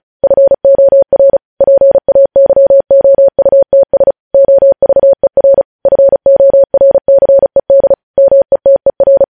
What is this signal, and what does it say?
Morse code, heard as a keyed tone, about 36 wpm, 550 Hz. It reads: FOR PAYOUTS OVER FORCED METER
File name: podcast / News headlines-1x-35x35